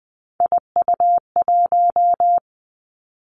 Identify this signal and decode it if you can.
Morse code, heard as a keyed tone, about 20 wpm, 700 Hz. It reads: IU1
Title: Three Challenging Letter-Number Combinations 20wpm